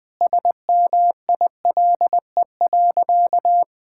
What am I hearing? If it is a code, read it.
Morse code, heard as a keyed tone, about 20 wpm, 700 Hz. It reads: SMILE.
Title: Sentences with 2-Word Phrases - 20wpm